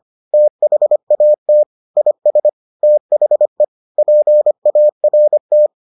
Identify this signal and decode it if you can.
Morse code, heard as a keyed tone, about 25 wpm, 600 Hz. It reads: THAT IS THE PART